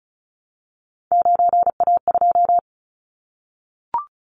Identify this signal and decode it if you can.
Morse code, heard as a keyed tone, about 34 wpm, 700 Hz. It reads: 9A2
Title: Callsign Prefixes for Top 25 DXCC - Repeated 3x - 35wpm